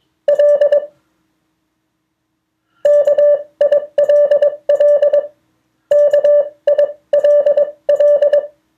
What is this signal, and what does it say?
Morse code, heard as a keyed tone, about 21 wpm, spaced as for 18 wpm, 580 Hz. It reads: L KILL KILL
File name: 2of4-words-1